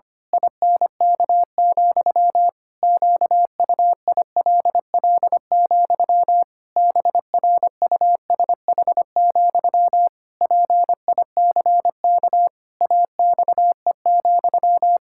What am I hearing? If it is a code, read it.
Morse code, heard as a keyed tone, about 25 wpm, 700 Hz. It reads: INK, QUILL, BRUSH, PICK AXE,
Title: Pangram Challenge v2 - 25wpm